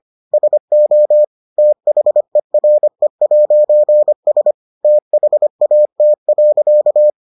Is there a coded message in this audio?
Yes: Morse code from 0.3 to 7.1 s, heard as a keyed tone, about 25 wpm, 600 Hz. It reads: SO THERE'S THAT.